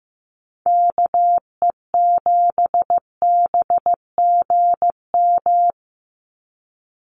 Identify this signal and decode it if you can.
Morse code, heard as a keyed tone, about 15 wpm, 700 Hz. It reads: KE7BGM
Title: NA SOTA Activator Callsigns 15wpm